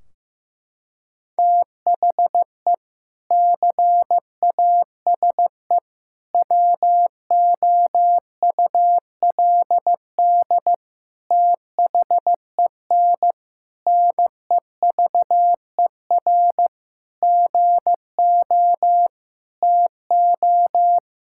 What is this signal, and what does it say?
Morse code, heard as a keyed tone, about 16 wpm, 700 Hz. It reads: THE CASE WOULD THEN NEVER GO TO
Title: Sentences from Top 1600 Words 15wpm